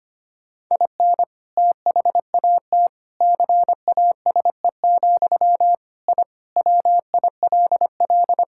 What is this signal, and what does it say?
Morse code, heard as a keyed tone, about 25 wpm, 700 Hz. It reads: IN THAT CASE, I WILL
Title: Sentences from Top 1000 Words - Review Part 6 - 25wpm